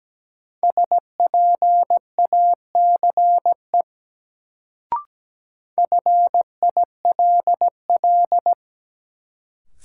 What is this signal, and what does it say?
Morse code, heard as a keyed tone, about 17 wpm, 700 Hz. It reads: SPACE FILL
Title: Sentences from Top 1500 Words - New Word Drill Alt - 17wpm